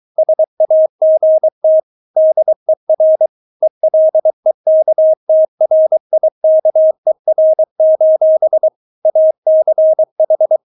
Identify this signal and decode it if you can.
Morse code, heard as a keyed tone, about 23 wpm, 630 Hz. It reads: SAGT DER ELEKTRIKER: ACH
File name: Witz4 23wpm